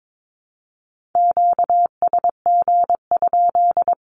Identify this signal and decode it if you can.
Morse code, heard as a keyed tone, about 22 wpm, 700 Hz. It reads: QSG?
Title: Q-Codes 22wpm